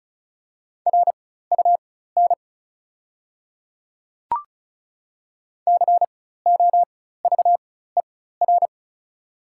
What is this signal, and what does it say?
Morse code, heard as a keyed tone, about 36 wpm, spaced as for 16 wpm, 690 Hz. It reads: RUN COVER
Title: 4x Character Spacing - Top 300 Words - 35wpm